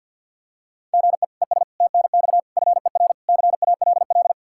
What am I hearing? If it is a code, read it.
Morse code, heard as a keyed tone, about 52 wpm, 700 Hz. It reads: GE ES TNX FER CALL
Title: Practice QSO - Exchange 1 - Beginning - 50wpm